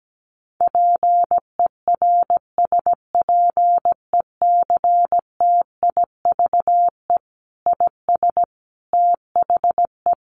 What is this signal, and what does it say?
Morse code, heard as a keyed tone, about 17 wpm, 700 Hz. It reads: PERSPECTIVE IS THE
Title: Sentences from Top 1200 Words 17wpm